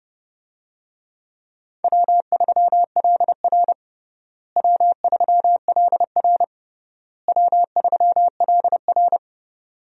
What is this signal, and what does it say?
Morse code, heard as a keyed tone, about 32 wpm, 700 Hz. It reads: W3LR W3LR W3LR
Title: USA Call Signs - 4 Characters - Repeated 3x - 30wpm